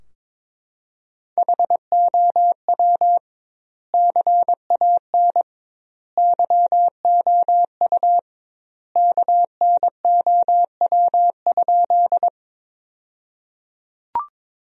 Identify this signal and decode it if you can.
Morse code, heard as a keyed tone, about 22 wpm, 700 Hz. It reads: HOW CAN YOU KNOW?
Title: 2x Word-Spacing - Sentences from Top 100 Words - 22wpm